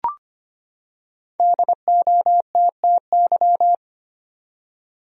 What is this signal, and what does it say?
Morse code, heard as a keyed tone, about 25 wpm, 700 Hz. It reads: DOTTY
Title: USA Common Names - 25wpm